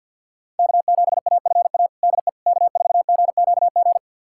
Morse code, heard as a keyed tone, about 53 wpm, 700 Hz. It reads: K7AFA DE K4GQG
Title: USA Callsign DE Callsign - 5 Characters - 50wpm